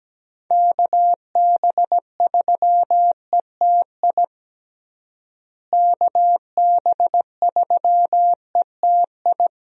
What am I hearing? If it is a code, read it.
Morse code, heard as a keyed tone, about 17 wpm, 700 Hz. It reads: KB3ETI KB3ETI
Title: USA Call Signs - 6 Characters - Repeated 3x - 17wpm